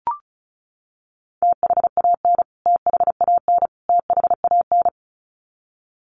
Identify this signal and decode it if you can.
Morse code, heard as a keyed tone, about 34 wpm, 700 Hz. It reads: THAN THAN THAN
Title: Top 300 Words Repeated 35wpm